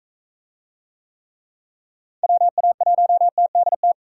Morse code, heard as a keyed tone, about 40 wpm, 700 Hz. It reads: WA1TDT